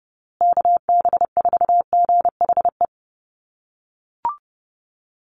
Morse code, heard as a keyed tone, about 30 wpm, 700 Hz. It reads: KB4GHE
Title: USA Call Signs Repeated 3x 30wpm